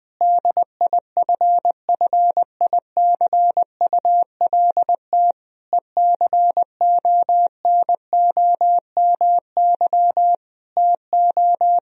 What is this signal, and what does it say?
Morse code, heard as a keyed tone, about 20 wpm, 700 Hz. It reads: DIFFICULT ECONOMY TO
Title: Sentences from Top 1000 Words - Review Part 3 - 20wpm